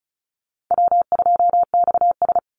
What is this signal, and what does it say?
Morse code, heard as a keyed tone, about 34 wpm, 700 Hz. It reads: W2XS
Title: Call Signs Repeated 3x - 35wpm